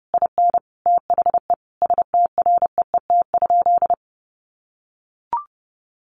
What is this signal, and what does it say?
Morse code, heard as a keyed tone, about 30 wpm, 700 Hz. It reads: IN THE STREET?
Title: Sentences from Top 900 Words 30wpm